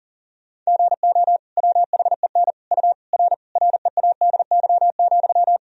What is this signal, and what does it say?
Morse code, heard as a keyed tone, about 40 wpm, 700 Hz. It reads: GO WHEN U R READY,
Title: Sentences from Top 500 Words and Abbreviations 40wpm